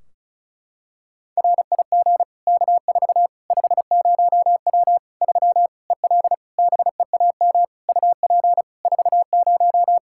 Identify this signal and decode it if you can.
Morse code, heard as a keyed tone, about 34 wpm, 700 Hz. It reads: RIG K4 50W 3 EL BEAM UP 40
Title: Practice QSO - Exchange 2 - Info - 35wpm